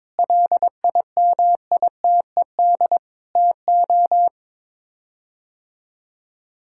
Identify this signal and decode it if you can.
Morse code, heard as a keyed tone, about 22 wpm, 700 Hz. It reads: LIMITED TO
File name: practice / 22wpm / Sets of 6 Words - 22wpm